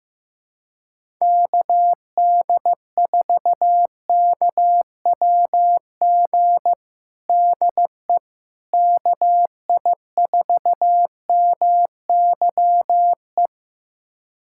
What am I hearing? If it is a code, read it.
Morse code, heard as a keyed tone, about 15 wpm, 700 Hz. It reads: KD4KWG DE KI4MYE
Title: USA Callsign DE Callsign 15wpm